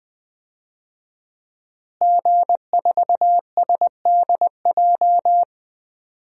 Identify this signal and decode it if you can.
Morse code, heard as a keyed tone, about 20 wpm, 700 Hz. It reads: G4SDJ
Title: DXCC-100 Call Signs - 5 Characters - Repeated 2x - 20wpm